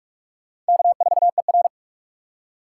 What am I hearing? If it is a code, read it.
Morse code, heard as a keyed tone, about 46 wpm, 700 Hz. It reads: K4ER
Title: USA Call Signs - 4 Characters - 4x Character Spacing - 45wpm